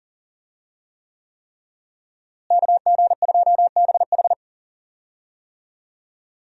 Morse code, heard as a keyed tone, about 37 wpm, 700 Hz. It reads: KG2BH
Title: USA Call Signs - 5 Characters - Repeated 3x - 40wpm